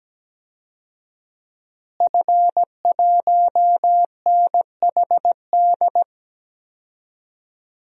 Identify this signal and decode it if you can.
Morse code, heard as a keyed tone, about 17 wpm, 700 Hz. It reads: F1NHD
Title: DXCC-100 Call Signs - 5 Characters - Repeated 2x - 17wpm